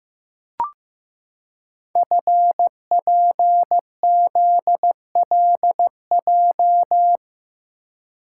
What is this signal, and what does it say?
Morse code, heard as a keyed tone, about 15 wpm, 700 Hz. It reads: FPZLJ
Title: Code Groups - 5 Letter:Numbers - 15wpm